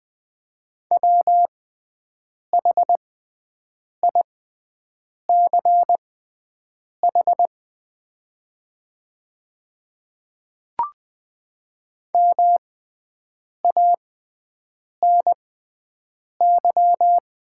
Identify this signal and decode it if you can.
Morse code, heard as a keyed tone, about 20 wpm, spaced as for 7 wpm, 700 Hz. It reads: WHICH MANY
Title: 6x Character Spacing - Top 200 Words - 20wpm